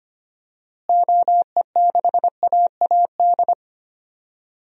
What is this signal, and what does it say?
Morse code, heard as a keyed tone, about 25 wpm, 700 Hz. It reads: OE6AAD
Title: DXCC-100 Call Signs - 6 Characters - Repeated 3x - 25wpm